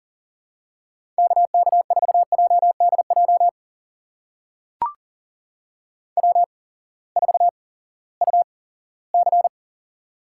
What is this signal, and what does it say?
Morse code, heard as a keyed tone, about 40 wpm, spaced as for 20 wpm, 700 Hz. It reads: KK4JDJ W4UC
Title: USA Call Signs - 8x Character Spacing - 40wpm